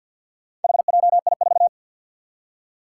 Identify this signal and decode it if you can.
Morse code, heard as a keyed tone, about 51 wpm, 700 Hz. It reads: HJI4
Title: Four Challenging Letter-Number Combinations 50wpm